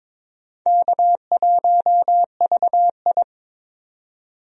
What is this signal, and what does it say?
Morse code, heard as a keyed tone, about 23 wpm, 700 Hz. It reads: K1VI
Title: USA Call Signs - 4 Characters - 22wpm